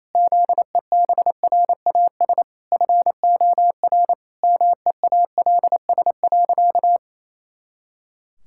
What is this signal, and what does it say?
Morse code, heard as a keyed tone, about 27 wpm, 700 Hz. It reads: ZEBRAS FOR MEALS.
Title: Pangram Challenge v2 - 28wpm